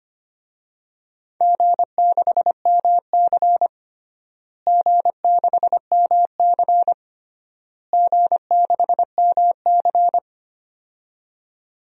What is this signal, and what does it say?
Morse code, heard as a keyed tone, about 25 wpm, 700 Hz. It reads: G6MC G6MC G6MC
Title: DXCC-100 Call Signs - 4 Characters - Repeated 3x - 25wpm